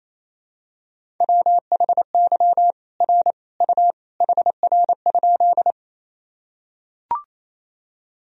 Morse code, heard as a keyed tone, about 28 wpm, 700 Hz. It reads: WHY R U HR?
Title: Sentences from Top 200 Words and Abbreviations 28wpm